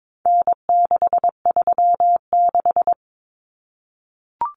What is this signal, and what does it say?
Morse code, heard as a keyed tone, about 22 wpm, 700 Hz. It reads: N636
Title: Common Cut Numbers - 4-Digit 22wpm